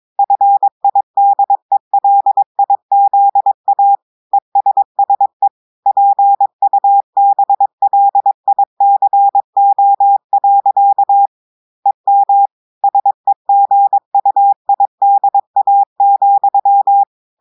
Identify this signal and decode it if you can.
Morse code, heard as a keyed tone, about 22 wpm, 800 Hz. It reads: FIDELIZA ESSE PUBLICO. EM SEGUIDA,
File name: audios / portugais / portugais texte en clair 2 vitesse 22 mots minutes